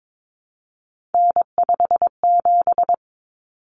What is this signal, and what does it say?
Morse code, heard as a keyed tone, about 22 wpm, 700 Hz. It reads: N57
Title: Common Cut Numbers - 3-Digit 22wpm